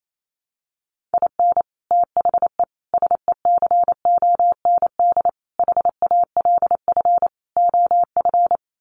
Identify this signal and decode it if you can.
Morse code, heard as a keyed tone, about 28 wpm, 700 Hz. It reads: IN THE SECOND HALF OF
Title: Sets of 5 Words 28wpm